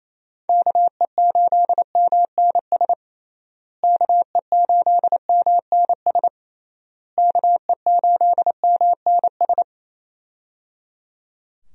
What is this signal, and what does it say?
Morse code, heard as a keyed tone, about 28 wpm, 700 Hz. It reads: KE8MNS KE8MNS KE8MNS
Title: USA Call Signs - 6 Characters - Repeated 3x - 28wpm